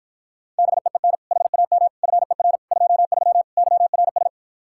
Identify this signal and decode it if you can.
Morse code, heard as a keyed tone, about 52 wpm, 700 Hz. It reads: BEEN HAM FER 23 YRS